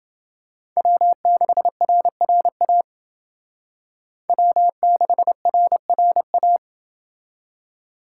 Morse code, heard as a keyed tone, about 28 wpm, 700 Hz. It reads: W6RRA W6RRA